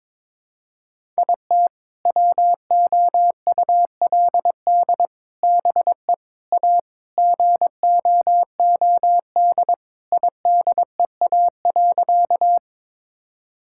Morse code, heard as a keyed tone, about 22 wpm, 700 Hz. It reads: IT WOULD BE A GOOD IDEA.
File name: practice / 22wpm / Sentences with 3-Word Phrases - 22wpm